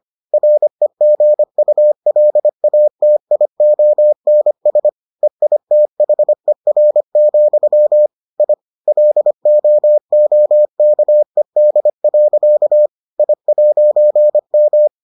Morse code, heard as a keyed tone, about 25 wpm, 600 Hz. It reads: REGULATIONS EITHER, I LOOKED. I'M